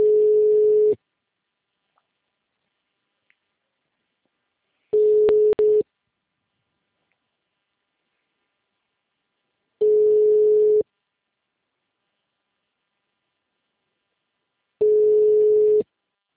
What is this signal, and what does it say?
Ringback tone (Europe, 425 Hz).
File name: out- -2002- - - .12840